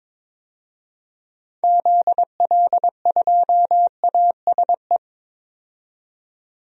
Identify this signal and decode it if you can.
Morse code, heard as a keyed tone, about 21 wpm, 700 Hz. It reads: ZL2ASE
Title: DXCC-100 Call Signs - 6 Characters - 22wpm